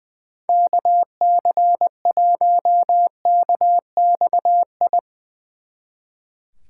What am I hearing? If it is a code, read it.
Morse code, heard as a keyed tone, about 23 wpm, 700 Hz. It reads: KC1KXI
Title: Speed Racing - USA 2x3 Call Signs 20wpm